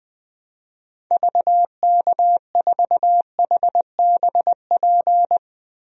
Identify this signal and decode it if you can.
Morse code, heard as a keyed tone, about 20 wpm, 700 Hz. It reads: VK4HBP
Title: DXCC-100 Call Signs - 6 Characters - 20wpm